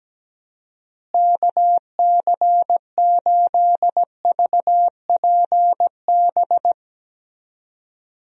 Morse code, heard as a keyed tone, about 17 wpm, 700 Hz. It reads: KC8VPB
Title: USA Call Signs - 6 Characters - Repeated 2x - 17wpm